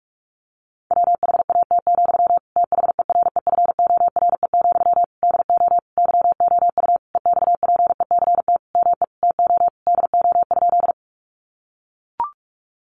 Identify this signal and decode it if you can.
Morse code, heard as a keyed tone, about 44 wpm, 700 Hz. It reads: WHAT, THEREFORE, DO YOU EXPECT ME TO DO?
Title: Sentences from Top 1100 Words 45wpm